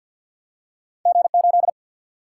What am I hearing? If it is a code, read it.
Morse code, heard as a keyed tone, about 49 wpm, 700 Hz. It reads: G8
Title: Single-Letter Single-Number 50wpm